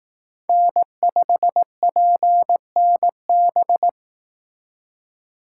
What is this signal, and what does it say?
Morse code, heard as a keyed tone, about 18 wpm, 700 Hz. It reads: N5PNB
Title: Speed Racing - USA 1x3 Call Signs 15wpm